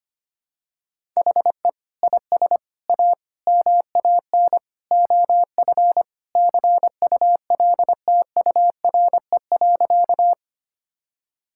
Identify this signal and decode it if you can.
Morse code, heard as a keyed tone, about 25 wpm, 700 Hz. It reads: HE IS A MAN OF CULTURE.
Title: Sentences from Top 1000 Words - Review Part 6 - 25wpm